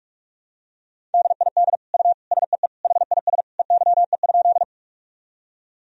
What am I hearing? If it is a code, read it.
Morse code, heard as a keyed tone, about 46 wpm, 700 Hz. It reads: DID U SEE HIS EYE?
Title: Sentences from Top 300 Words and Abbreviations 45wpm